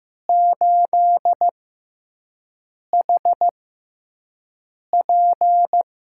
Morse code, heard as a keyed tone, about 15 wpm, spaced as for 5 wpm, 700 Hz. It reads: 8HP
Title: USA Call Signs - 5 Characters - 6x Character Spacing - 15wpm